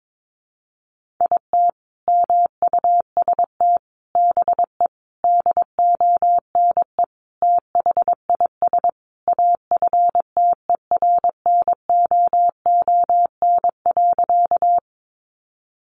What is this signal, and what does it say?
Morse code, heard as a keyed tone, about 22 wpm, 700 Hz. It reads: IT MUST BE DONE THIS AFTERNOON.
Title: Sentences from Top 1500 Words 22wpm